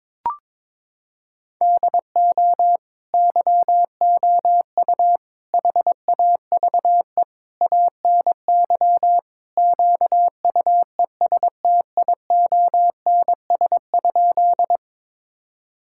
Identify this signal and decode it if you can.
Morse code, heard as a keyed tone, about 22 wpm, 700 Hz. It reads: DO YOU HAVE ANY QUESTIONS?
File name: practice / 22wpm / Sentences with 4-Word Phrases - 22wpm